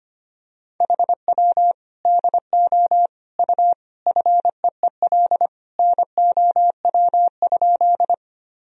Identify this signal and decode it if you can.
Morse code, heard as a keyed tone, about 25 wpm, 700 Hz. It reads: HW DO U FEEL NOW?
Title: Sentences from Top 400 Words and Abbreviations 25wpm